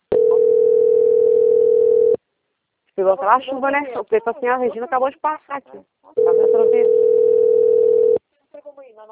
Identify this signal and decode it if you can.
Ringback tone (North America), 440 + 480 Hz, with speech.
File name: rg-1600-unknown- - - .52211